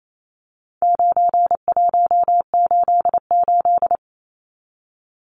Morse code, heard as a keyed tone, about 28 wpm, 700 Hz. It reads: 9188
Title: Numbers - 4-Digit 28wpm